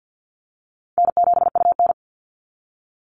Morse code, heard as a keyed tone, about 48 wpm, 700 Hz. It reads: N7UN
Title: NA SOTA Activator Callsigns 50wpm